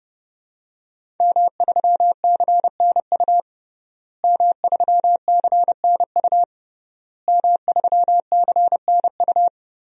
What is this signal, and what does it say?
Morse code, heard as a keyed tone, about 31 wpm, 700 Hz. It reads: M3CNU M3CNU M3CNU
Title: DXCC-100 Call Signs - 5 Characters - Repeated 3x - 30wpm